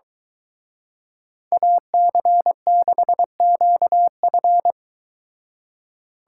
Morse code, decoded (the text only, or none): AC6QF